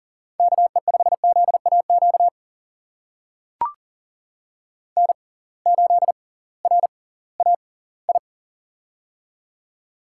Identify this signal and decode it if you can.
Morse code, heard as a keyed tone, about 38 wpm, spaced as for 21 wpm, 700 Hz. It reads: KE5ZAQ N8RAI
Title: USA Call Signs - 6x Character Spacing - 40wpm